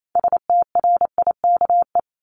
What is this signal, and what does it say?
Morse code, heard as a keyed tone, about 28 wpm, 700 Hz. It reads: STRIKE
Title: Top 1501-1750 Words 28wpm